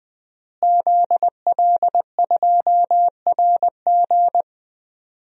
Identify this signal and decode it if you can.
Morse code, heard as a keyed tone, about 21 wpm, 700 Hz. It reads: ZL2RG
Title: DXCC-100 Call Signs - 5 Characters - Repeated 3x - 20wpm